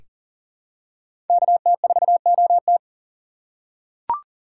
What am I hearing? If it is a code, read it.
Morse code, heard as a keyed tone, about 40 wpm, 700 Hz. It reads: KT4OT
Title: USA Call Signs - 5 Characters - 3x Character Spacing - 40wpm